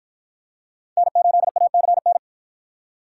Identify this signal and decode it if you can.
Morse code, heard as a keyed tone, about 53 wpm, 700 Hz. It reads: N8ACN